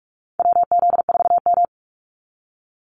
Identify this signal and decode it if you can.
Morse code, heard as a keyed tone, about 44 wpm, 700 Hz. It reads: WZ4M
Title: NA SOTA Activator Callsigns 45wpm